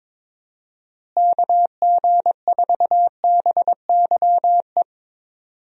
Morse code, heard as a keyed tone, about 22 wpm, 700 Hz. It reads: KG4BYE
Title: USA Call Signs - 6 Characters - Repeated 3x - 22wpm